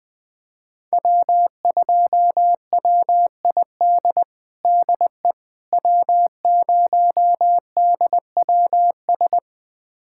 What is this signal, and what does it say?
Morse code, heard as a keyed tone, about 20 wpm, 700 Hz. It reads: W2WID DE W0DWS
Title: USA Callsign DE Callsign - 5 Characters - 20wpm